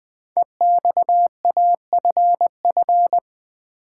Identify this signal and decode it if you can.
Morse code, heard as a keyed tone, about 20 wpm, 700 Hz. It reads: EXAFF
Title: Code Groups - 5 Letters - 20wpm